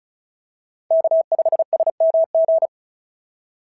Morse code, heard as a keyed tone, about 36 wpm, 690 Hz. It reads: K5SMG
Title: SOTA QSO Practice 35wpm